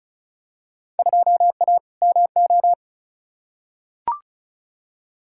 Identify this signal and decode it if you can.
Morse code, heard as a keyed tone, about 36 wpm, 700 Hz. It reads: 2A MO
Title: Field Day Exchanges - 35wpm